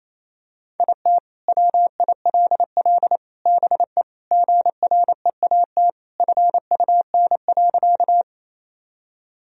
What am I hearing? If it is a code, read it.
Morse code, heard as a keyed tone, about 28 wpm, 700 Hz. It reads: IT WILL BE GREAT FUN.
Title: Sentences from Top 1000 Words - Review Part 4 - 28wpm